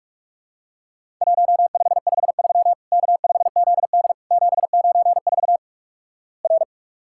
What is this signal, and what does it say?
Morse code, heard as a keyed tone, about 48 wpm, 700 Hz. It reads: 1553 K5ZD 794 R
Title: Worked All Europe DX Contest - QTC Exchange - 45wpm